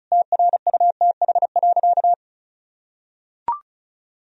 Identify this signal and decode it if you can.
Morse code, heard as a keyed tone, about 36 wpm, 700 Hz. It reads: TRUTH.
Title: Sentences from Top 1000 Words - Review Part 2 - 35wpm